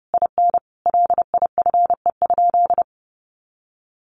Morse code, heard as a keyed tone, about 28 wpm, 700 Hz. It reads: IN LIFE?
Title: Sentences from Top 1400 Words 30wpm